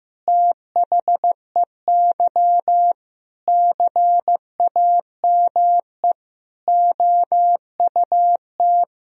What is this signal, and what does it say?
Morse code, heard as a keyed tone, about 15 wpm, 700 Hz. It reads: THEY CAME OUT
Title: Sentences from Top 1000 Words - Review Part 4 - 15wpm